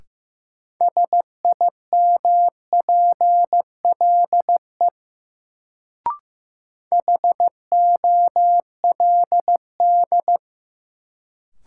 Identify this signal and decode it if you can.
Morse code, heard as a keyed tone, about 15 wpm, 700 Hz. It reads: SIMPLE HOLD
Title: Sentences from Top 400 Words - New Word Drill - 15wpm